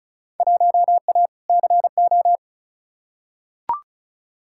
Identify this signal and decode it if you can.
Morse code, heard as a keyed tone, about 36 wpm, 700 Hz. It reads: 1A CO
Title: Field Day Exchanges - 35wpm